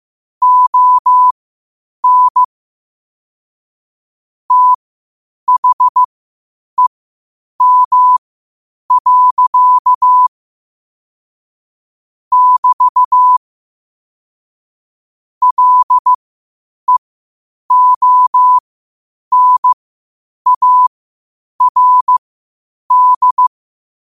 Morse code, heard as a keyed tone, about 15 wpm, spaced as for 8 wpm, 1000 Hz. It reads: ON THEM. <BT> LEONARD
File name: qotdfiles / 8wpm